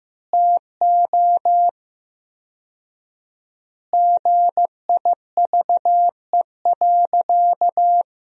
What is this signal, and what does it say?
Morse code, heard as a keyed tone, about 15 wpm, 700 Hz. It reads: TO GIVE.
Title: 4x Word-Spacing - Sentences from Top 200 Words - 15wpm